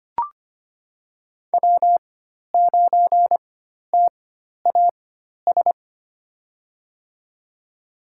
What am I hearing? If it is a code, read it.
Morse code, heard as a keyed tone, about 25 wpm, spaced as for 12 wpm, 700 Hz. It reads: W9TAS
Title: USA Call Signs - 4x Character Spacing - 25wpm